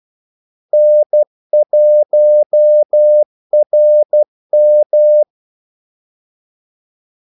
Morse code, heard as a keyed tone, about 12 wpm, 600 Hz. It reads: N1RM
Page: 2 ekstra ordmellemrum mellem kaldesignalerne